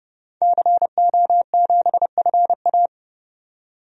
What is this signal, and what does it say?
Morse code, heard as a keyed tone, about 31 wpm, 700 Hz. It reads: CO7FA
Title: DXCC-100 Call Signs - 5 Characters - 30wpm